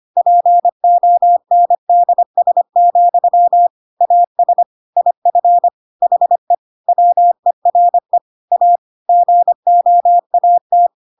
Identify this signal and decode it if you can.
Morse code, heard as a keyed tone, about 25 wpm, 700 Hz. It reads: PONDS, AS IF HE WERE A GOAT
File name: Pinocchio / Pinocchio ch4 25 25 ik1hge